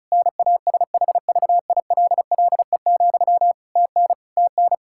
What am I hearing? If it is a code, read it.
Morse code, heard as a keyed tone, about 36 wpm, 700 Hz. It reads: NASHVILLE, TN TN
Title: Practice QSO - Exchange 1 - Beginning - 35wpm